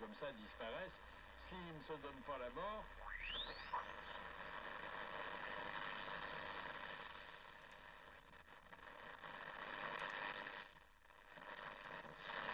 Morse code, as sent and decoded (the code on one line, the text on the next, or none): none